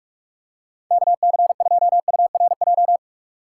Morse code, heard as a keyed tone, about 46 wpm, 700 Hz. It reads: KC2URJ
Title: USA Call Signs - 6 Characters - 3x Character Spacing - 45wpm